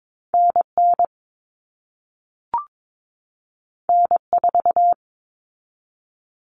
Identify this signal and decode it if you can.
Morse code, heard as a keyed tone, about 22 wpm, 700 Hz. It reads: NN N4
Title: All Cut Numbers - 2-Digit 22wpm